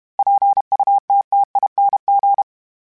Morse code, heard as a keyed tone, about 32 wpm, 800 Hz. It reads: PUTTING
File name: Special INGs-mp3s / Special INGs-mp3s / Sngl Ing @30